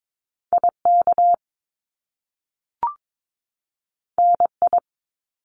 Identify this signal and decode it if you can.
Morse code, heard as a keyed tone, about 22 wpm, 700 Hz. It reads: IK NI
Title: Top Two Letter Combinations in Words 22wpm